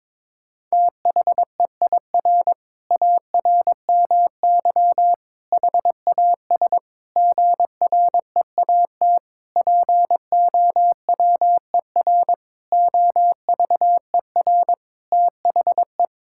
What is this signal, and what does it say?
Morse code, heard as a keyed tone, about 22 wpm, 700 Hz. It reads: THEIR ARMY HAS GREAT POWER OVER THE
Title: Sentences from Top 1000 Words - Review Part 5 - 22wpm